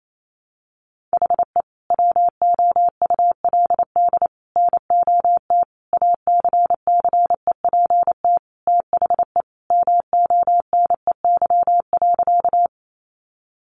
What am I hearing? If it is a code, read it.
Morse code, heard as a keyed tone, about 28 wpm, 700 Hz. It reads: HE WOULD NOT ACCEPT THE MONEY.
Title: Sentences from Top 1300 Words 28wpm